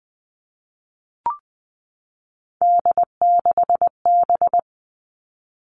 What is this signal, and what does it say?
Morse code, heard as a keyed tone, about 19 wpm, 700 Hz. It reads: D6B
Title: Three Challenging Letter-Number Combinations 20wpm